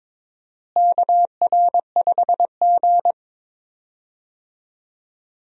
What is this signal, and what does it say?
Morse code, heard as a keyed tone, about 22 wpm, 700 Hz. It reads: KR5G
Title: USA Call Signs - 4 Characters - Repeated 2x - 22wpm